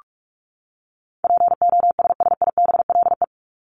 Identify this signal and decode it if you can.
Morse code, heard as a keyed tone, about 44 wpm, 700 Hz. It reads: POSSIBLE
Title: Top 500 Words 45wpm